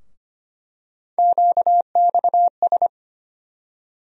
Morse code, heard as a keyed tone, about 25 wpm, 700 Hz. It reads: QXS
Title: USA Call Sign Postfixes - 3 Letters 25wpm